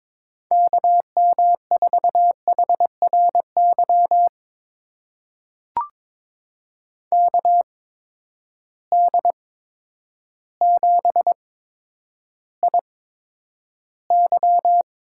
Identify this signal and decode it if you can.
Morse code, heard as a keyed tone, about 22 wpm, spaced as for 10 wpm, 700 Hz. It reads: KM4HRY KD7IY
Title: USA Call Signs - 6 Characters - 8x Character Spacing - 22wpm